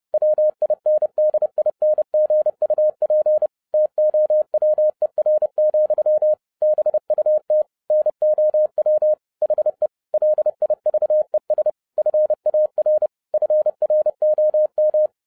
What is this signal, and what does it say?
Morse code, heard as a keyed tone, about 30 wpm, 610 Hz. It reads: WINDINGUP TOWER, BUT NOW HE LIVES FAR FROM